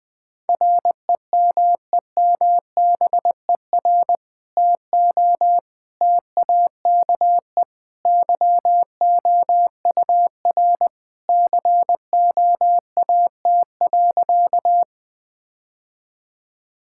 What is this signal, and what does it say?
Morse code, heard as a keyed tone, about 21 wpm, 700 Hz. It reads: REMEMBER TO TAKE YOUR COAT.
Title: Sentences with 2-Word Phrases - 20wpm